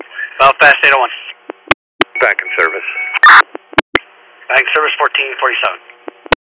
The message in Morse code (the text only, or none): none